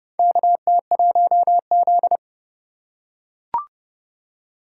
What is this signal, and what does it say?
Morse code, heard as a keyed tone, about 30 wpm, 700 Hz. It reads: KT1Z
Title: USA Call Signs - 4 Characters - 3x Character Spacing - 30wpm